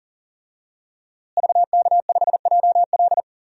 Morse code, heard as a keyed tone, about 40 wpm, 700 Hz. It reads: VK5JL
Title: DXCC-100 Call Signs - 5 Characters - Repeated 3x - 40wpm